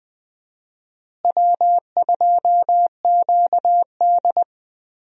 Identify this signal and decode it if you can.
Morse code, heard as a keyed tone, about 20 wpm, 700 Hz. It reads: W2QD
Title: USA Call Signs - 4 Characters - Repeated 3x - 20wpm